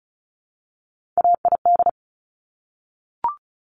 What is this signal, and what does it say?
Morse code, heard as a keyed tone, about 36 wpm, 700 Hz. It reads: AID
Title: Top Three Letter Combinations in Words 35wpm